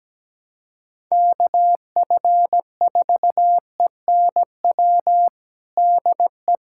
Morse code, heard as a keyed tone, about 17 wpm, 700 Hz. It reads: KF4ENW DE
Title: USA Callsign DE Callsign - 6 Characters - 17wpm